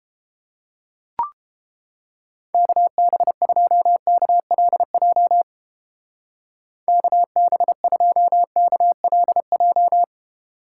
Morse code, heard as a keyed tone, about 29 wpm, 700 Hz. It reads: KB2KLJ KB2KLJ